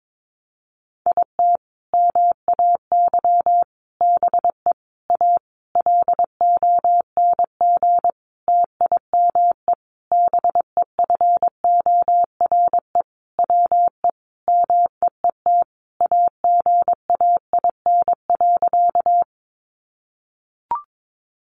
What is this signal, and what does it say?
Morse code, heard as a keyed tone, about 22 wpm, 700 Hz. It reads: IT MAY BE A LONG TIME BEFORE WE MEET AGAIN.
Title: Sentences from Top 1100 Words 22wpm